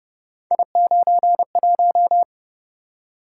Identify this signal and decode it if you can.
Morse code, heard as a keyed tone, about 28 wpm, 700 Hz. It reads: I91